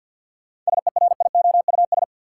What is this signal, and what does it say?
Morse code, heard as a keyed tone, about 53 wpm, 700 Hz. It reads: SERIOUS
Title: Sentences from Top 1100 Words - New Word Drill Alt - 50wpm